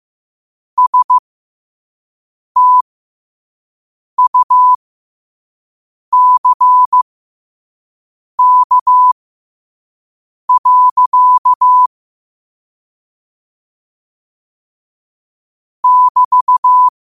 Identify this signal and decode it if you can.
Morse code, heard as a keyed tone, about 15 wpm, spaced as for 5 wpm, 1000 Hz. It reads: STUCK. <BT>
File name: qotdfiles / 5wpm